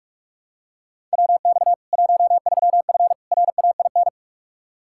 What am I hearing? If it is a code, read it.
Morse code, heard as a keyed tone, about 46 wpm, 700 Hz. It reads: WX 13F RAIN
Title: Practice QSO - Exchange 2 - Info - 45wpm